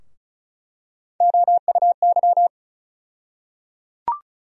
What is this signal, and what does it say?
Morse code, heard as a keyed tone, about 34 wpm, 700 Hz. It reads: OUY
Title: USA Call Sign Postfixes - 3 Letters 35wpm